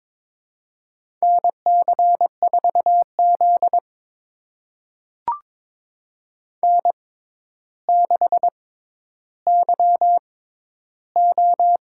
Morse code, heard as a keyed tone, about 21 wpm, spaced as for 11 wpm, 700 Hz. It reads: NC4Z N6YO
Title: USA Call Signs - 4 Characters - 6x Character Spacing - 22wpm